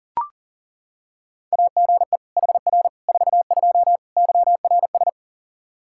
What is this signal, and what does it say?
Morse code, heard as a keyed tone, about 40 wpm, 700 Hz. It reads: AGE HR 42 YRS
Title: Practice QSO - Exchange 3 - More Info - 40wpm